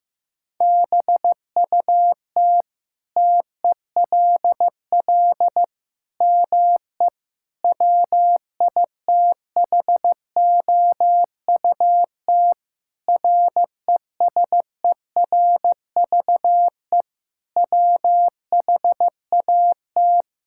Morse code, heard as a keyed tone, about 15 wpm, 700 Hz. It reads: BUT TELL ME WITHOUT RESERVE WHAT
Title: Sentences from Top 1000 Words - Review Part 6 - 15wpm